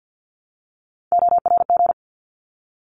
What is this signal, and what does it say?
Morse code, heard as a keyed tone, about 48 wpm, 700 Hz. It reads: ORG
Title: ARRL Sections 50wpm